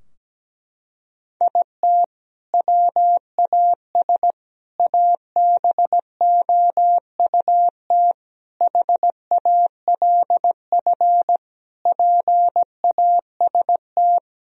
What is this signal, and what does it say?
Morse code, heard as a keyed tone, about 17 wpm, 700 Hz. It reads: IT WAS ABOUT HALF PAST
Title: Sentences from Top 1000 Words - Review Part 3 - 17wpm